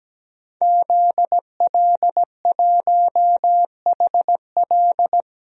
Morse code, heard as a keyed tone, about 17 wpm, 700 Hz. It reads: ZL1HL
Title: DXCC-100 Call Signs - 5 Characters - 17wpm